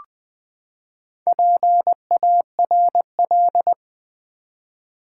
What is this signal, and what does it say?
Morse code, heard as a keyed tone, about 20 wpm, 700 Hz. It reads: PARL
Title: Build A Word - 4 Letters At A Time - 10-Letter Words - 20wpm